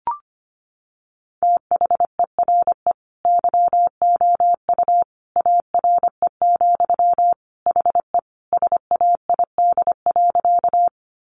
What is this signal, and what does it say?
Morse code, heard as a keyed tone, about 25 wpm, 700 Hz. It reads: THERE YOU ARE, HE SAID.
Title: Sentences from Top 100 Words 25wpm